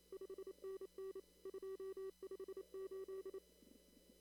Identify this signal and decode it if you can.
Morse code, heard as a keyed tone, about 26 wpm, 380 Hz. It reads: 5NN 258
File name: CW / kt5j wpx cw-op-k5pi